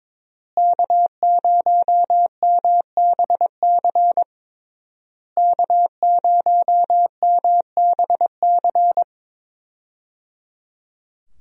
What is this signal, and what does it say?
Morse code, heard as a keyed tone, about 22 wpm, 700 Hz. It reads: K0MBC K0MBC
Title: USA Call Signs - 5 Characters - Repeated 3x - 22wpm